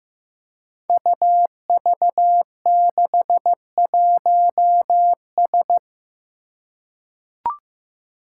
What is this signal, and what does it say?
Morse code, heard as a keyed tone, about 15 wpm, 700 Hz. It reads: UV61S
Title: Five Challenging Letter-Number Combinations 15wpm